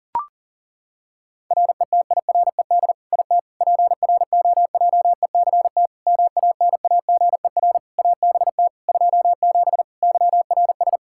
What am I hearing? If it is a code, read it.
Morse code, heard as a keyed tone, about 40 wpm, 700 Hz. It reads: RETIRED IT PROJECT MANAGER ABT 27 YRS
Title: Practice QSO - Exchange 3 - More Info - 40wpm